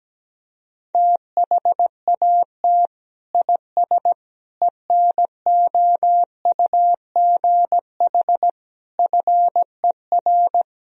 Morse code, heard as a keyed tone, about 17 wpm, 700 Hz. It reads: THAT IS ENOUGH FER
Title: Sentences from Top 400 Words and Abbreviations 17wpm